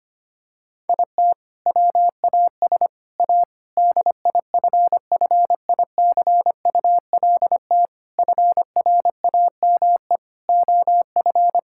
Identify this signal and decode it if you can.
Morse code, heard as a keyed tone, about 25 wpm, 700 Hz. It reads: IT WAS A DIFFICULT FRAME OF
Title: Sentences from Top 1000 Words - Review Part 6 - 25wpm